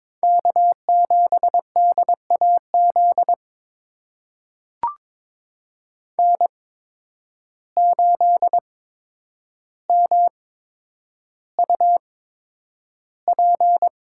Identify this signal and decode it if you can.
Morse code, heard as a keyed tone, about 21 wpm, spaced as for 9 wpm, 700 Hz. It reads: K7DAZ N8MUP
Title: USA Call Signs - 5 Characters - 8x Character Spacing - 22wpm